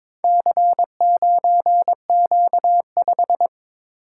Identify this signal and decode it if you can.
Morse code, heard as a keyed tone, about 22 wpm, 700 Hz. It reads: C9Q5
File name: practice / 22wpm / Four Letter-Number Combinations - 22wpm